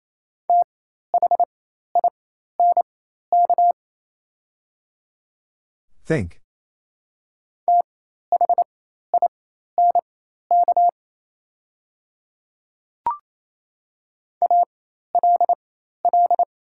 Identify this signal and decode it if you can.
Morse code, heard as a keyed tone, about 28 wpm, spaced as for 13 wpm, 700 Hz. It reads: THINK THINK ALL
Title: 4x Character Spacing - Top 300 Words - 28wpm